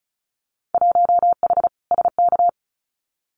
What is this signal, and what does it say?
Morse code, heard as a keyed tone, about 34 wpm, 700 Hz. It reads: 1H SK
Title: Winter Field Day Exchanges 35wpm